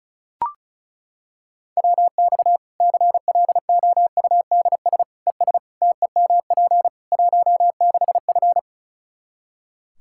Morse code, heard as a keyed tone, about 34 wpm, 700 Hz. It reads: WX CLOUDS ES TEMP 16F
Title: Practice QSO - Exchange 2 - Info - 35wpm